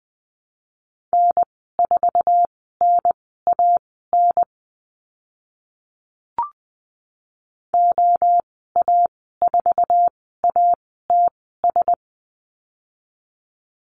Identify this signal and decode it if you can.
Morse code, heard as a keyed tone, about 20 wpm, spaced as for 14 wpm, 700 Hz. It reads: N4NAN OA4ATS
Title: 062 Lesson 11g - Letter h - Call Signs 20wpm